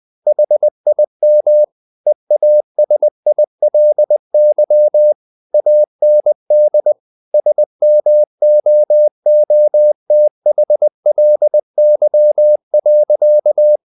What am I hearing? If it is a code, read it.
Morse code, heard as a keyed tone, about 20 wpm, 600 Hz. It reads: HIM EASILY AND SMOOTHLY.
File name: the old man and the sea / the old man and the sea part 1 20wpm